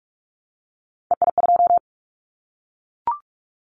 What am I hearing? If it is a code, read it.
Morse code, heard as a keyed tone, about 44 wpm, 700 Hz. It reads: EI2
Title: Callsign Prefixes for Top 50 DXCC - 45wpm